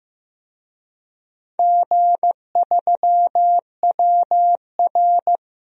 Morse code, heard as a keyed tone, about 15 wpm, 700 Hz. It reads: G3WR
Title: DXCC-100 Call Signs - 4 Characters - Repeated 3x - 15wpm